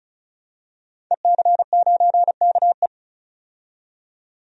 Morse code, heard as a keyed tone, about 34 wpm, 700 Hz. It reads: EC9KE
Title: DXCC-100 Call Signs - 5 Characters - 35wpm